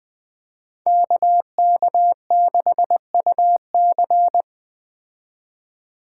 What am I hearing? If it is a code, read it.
Morse code, heard as a keyed tone, about 19 wpm, 700 Hz. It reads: KK6UC